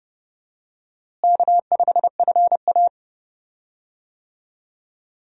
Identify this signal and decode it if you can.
Morse code, heard as a keyed tone, about 29 wpm, 700 Hz. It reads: K5FA
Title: USA Call Signs - 4 Characters - Repeated 2x - 30wpm